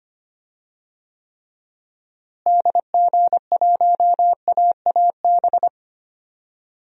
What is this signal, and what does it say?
Morse code, heard as a keyed tone, about 25 wpm, 700 Hz. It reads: DG1AAB
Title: DXCC-100 Call Signs - 6 Characters - Repeated 2x - 25wpm